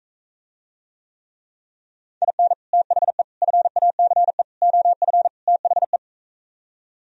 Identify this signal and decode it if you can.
Morse code, heard as a keyed tone, about 44 wpm, 700 Hz. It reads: IN THE FACE OF THE